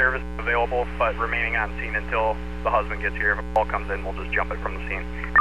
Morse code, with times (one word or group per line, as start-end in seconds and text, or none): none